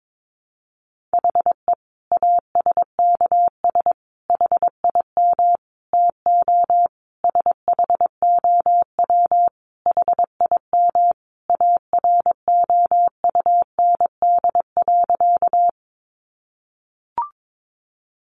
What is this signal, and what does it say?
Morse code, heard as a keyed tone, about 22 wpm, 700 Hz. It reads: HE ASKS HIM TO SHOW HIM AROUND.
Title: Easy Phrases 22wpm